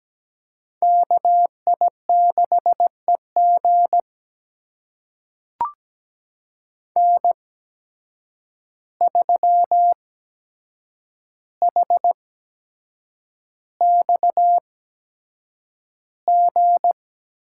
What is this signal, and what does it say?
Morse code, heard as a keyed tone, about 17 wpm, spaced as for 7 wpm, 700 Hz. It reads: KI6EG N3HXG
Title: USA Call Signs - 5 Characters - 8x Character Spacing - 17wpm